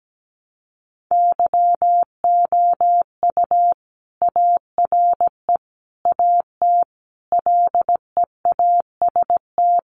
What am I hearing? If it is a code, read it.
Morse code, heard as a keyed tone, about 17 wpm, 700 Hz. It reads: YOU ARE AT LEAST
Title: Sentences from Top 1100 Words 17wpm